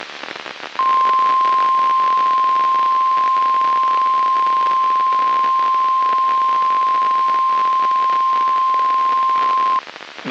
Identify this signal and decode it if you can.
Emergency Alert System (SAME) tones (the 1050 Hz attention tone), with noise. It is heard from 0.7 to 9.8 s.
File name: eas messages / EAS WXM95 Towanda